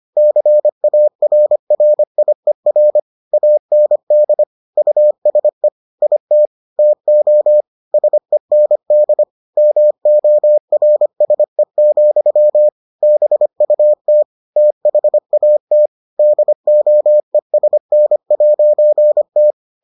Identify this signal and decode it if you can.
Morse code, heard as a keyed tone, about 25 wpm, 600 Hz. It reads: CARRIER AND USE IT TO SEND MORSE, BUT THAT DOESN'T